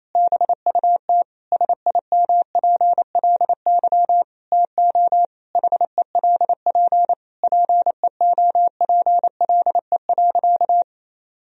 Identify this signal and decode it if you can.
Morse code, heard as a keyed tone, about 28 wpm, 700 Hz. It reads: BUT SIMPLY TO HELP PEOPLE.
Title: Sentences from Top 1000 Words - Review Part 4 - 28wpm